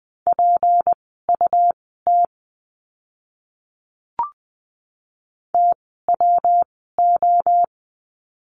Morse code, heard as a keyed tone, about 20 wpm, spaced as for 15 wpm, 700 Hz. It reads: PUT TWO
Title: 206 Lesson 33f - Letter j - Words 20wpm